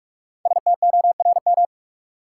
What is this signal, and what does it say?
Morse code, heard as a keyed tone, about 47 wpm, 700 Hz. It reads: STORM
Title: Sentences from Top 1500 Words - New Word Drill Alt - 45wpm